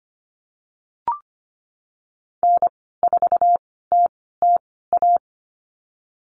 Morse code, heard as a keyed tone, about 25 wpm, spaced as for 16 wpm, 700 Hz. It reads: N4TTA
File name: Lesson 15g - Number 5 - Call Signs 25wpm